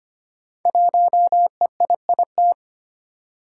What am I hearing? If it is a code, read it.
Morse code, heard as a keyed tone, about 25 wpm, 700 Hz. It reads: 1EIIT
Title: Five Challenging Letter-Number Combinations 25wpm